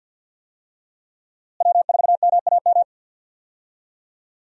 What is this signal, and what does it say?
Morse code, heard as a keyed tone, about 49 wpm, 700 Hz. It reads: W4MAM
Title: USA Call Signs - 5 Characters - Repeated 2x - 50wpm